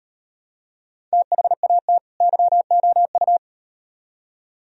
Morse code, heard as a keyed tone, about 38 wpm, 700 Hz. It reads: THAT YOU